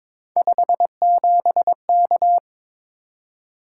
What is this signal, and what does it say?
Morse code, heard as a keyed tone, about 22 wpm, 700 Hz. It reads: 57K
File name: practice / 22wpm / Three Letter-Number Combinations - 22wpm